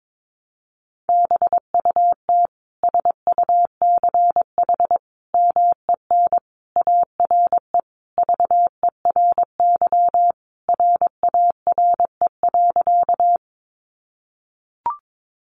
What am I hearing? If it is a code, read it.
Morse code, heard as a keyed tone, about 22 wpm, 700 Hz. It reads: BUT SUCH MEN ARE VERY RARE.
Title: Sentences from Top 1100 Words 22wpm